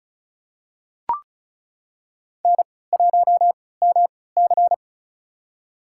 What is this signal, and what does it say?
Morse code, heard as a keyed tone, about 34 wpm, spaced as for 20 wpm, 700 Hz. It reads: N1MC
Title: USA Call Signs - 3x Character Spacing - 35wpm